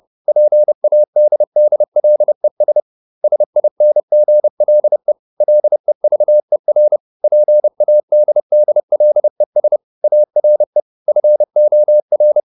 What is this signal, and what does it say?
Morse code, heard as a keyed tone, about 30 wpm, 600 Hz. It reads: PADDLES SINGLE LEVER PADDLES ARE FOR
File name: SS301 30